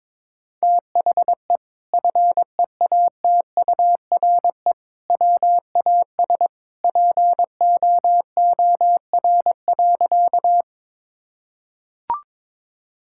Morse code, heard as a keyed tone, about 21 wpm, 700 Hz. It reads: THE FEATURE WAS POOR.
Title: Sentences from Top 1000 Words - Review Part 5 - 22wpm